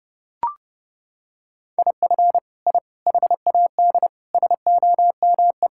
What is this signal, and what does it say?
Morse code, heard as a keyed tone, about 32 wpm, 700 Hz. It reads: IF I HAD SOME
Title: Sentences from Top 1000 Words - Review Part 5 - 30wpm